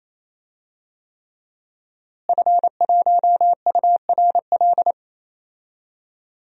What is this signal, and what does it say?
Morse code, heard as a keyed tone, about 28 wpm, 700 Hz. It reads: F1URL
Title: DXCC-100 Call Signs - 5 Characters - Repeated 3x - 28wpm